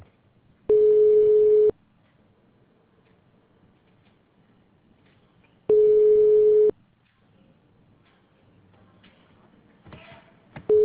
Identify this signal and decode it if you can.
Ringback tone (Europe, 425 Hz).